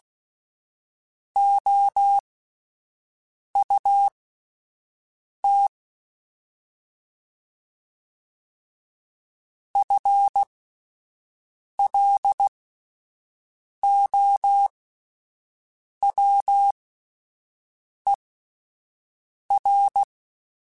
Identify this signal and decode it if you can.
Morse code, heard as a keyed tone, about 16 wpm, spaced as for 5 wpm, 800 Hz. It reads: OUT FLOWER
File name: Spaced / FL Runs sp mp3s / FL Runs sp @10